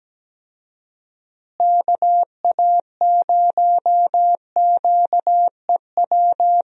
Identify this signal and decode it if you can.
Morse code, heard as a keyed tone, about 17 wpm, 700 Hz. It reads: KA0QEW
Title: USA Call Signs - 6 Characters - Repeated 2x - 17wpm